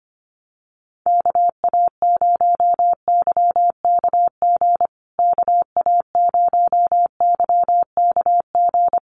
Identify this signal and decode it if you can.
Morse code, heard as a keyed tone, about 25 wpm, 700 Hz. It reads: KA0YKG KA0YKG
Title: USA Call Signs Repeated 3x 25wpm